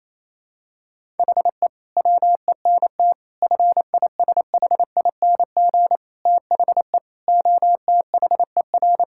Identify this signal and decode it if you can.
Morse code, heard as a keyed tone, about 27 wpm, 700 Hz. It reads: HE WENT FISHING THE OTHER
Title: Sentences from Top 1000 Words - Review Part 5 - 28wpm